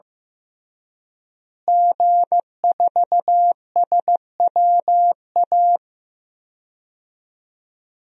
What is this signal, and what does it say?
Morse code, heard as a keyed tone, about 15 wpm, 700 Hz. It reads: G4SWA
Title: DXCC-100 Call Signs - 5 Characters - Repeated 3x - 15wpm